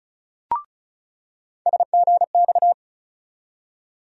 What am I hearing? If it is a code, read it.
Morse code, heard as a keyed tone, about 36 wpm, 700 Hz. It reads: SGX
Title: USA Call Sign Postfixes - 3 Letters 35wpm